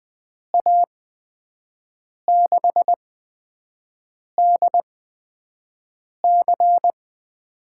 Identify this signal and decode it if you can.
Morse code, heard as a keyed tone, about 20 wpm, spaced as for 5 wpm, 700 Hz. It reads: A6DC
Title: USA Call Signs - 5 Characters - 8x Character Spacing - 20wpm